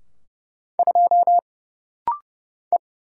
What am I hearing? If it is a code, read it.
Morse code, heard as a keyed tone, about 30 wpm, 700 Hz. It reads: 2 E
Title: Single Letter-Number Alt - Rapid-Fire 30wpm